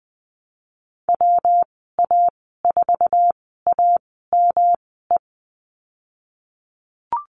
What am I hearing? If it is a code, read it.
Morse code, heard as a keyed tone, about 20 wpm, spaced as for 15 wpm, 700 Hz. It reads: WA4AME
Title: 122 Lesson 20g - Number 3 - Call Signs 20wpm